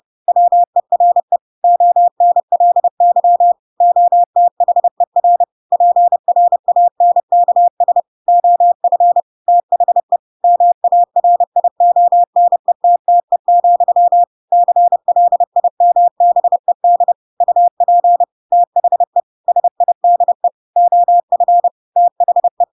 Morse code, heard as a keyed tone, about 30 wpm, 700 Hz. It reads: WERE ONLY OTHER PRANKS OF THE MARIONETTE, CLIMBED UP THE SIDE OF THE
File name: Pinocchio / Pinocchio ch7 30 30 ik1hge